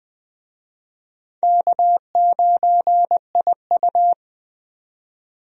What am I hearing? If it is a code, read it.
Morse code, heard as a keyed tone, about 20 wpm, 700 Hz. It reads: K9IU
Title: USA Call Signs - 4 Characters - 20wpm